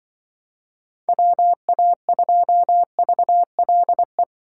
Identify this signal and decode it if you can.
Morse code, heard as a keyed tone, about 25 wpm, 700 Hz. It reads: WA2VLE
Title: Speed Racing - USA 2x3 Call Signs 20wpm